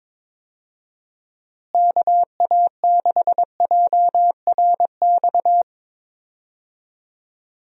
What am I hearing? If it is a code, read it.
Morse code, heard as a keyed tone, about 22 wpm, 700 Hz. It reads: KA6JRX
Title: USA Call Signs - 6 Characters - Repeated 3x - 22wpm